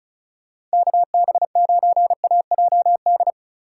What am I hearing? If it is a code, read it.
Morse code, heard as a keyed tone, about 36 wpm, 700 Hz. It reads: KB9AJD
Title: USA Call Signs - 4x Character Spacing - 35wpm